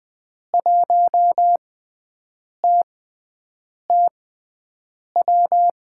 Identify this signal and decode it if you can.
Morse code, heard as a keyed tone, about 20 wpm, spaced as for 7 wpm, 700 Hz. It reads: 1TTW
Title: USA Call Signs - 6 Characters - 6x Character Spacing - 20wpm